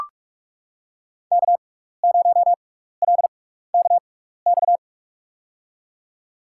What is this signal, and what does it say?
Morse code, heard as a keyed tone, about 44 wpm, spaced as for 15 wpm, 700 Hz. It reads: K0LKX
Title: USA Call Signs - 5 Characters - 6x Character Spacing - 45wpm